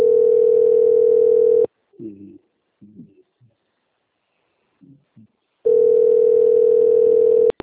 Ringback tone (North America), 440 + 480 Hz.
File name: out- - - - - .7710